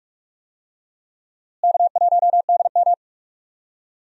Morse code, heard as a keyed tone, about 46 wpm, 700 Hz. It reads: K1DM
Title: USA Call Signs - 4 Characters - Repeated 2x - 45wpm